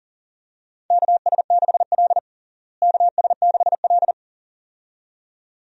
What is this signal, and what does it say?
Morse code, heard as a keyed tone, about 40 wpm, 700 Hz. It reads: KS6L KS6L
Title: USA Call Signs - 4 Characters - Repeated 2x - 40wpm